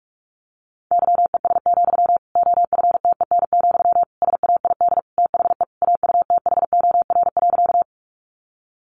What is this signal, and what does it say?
Morse code, heard as a keyed tone, about 44 wpm, 700 Hz. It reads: YES, OFTEN, SAID THE AUTHOR.
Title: Sentences from Top 1200 Words 45wpm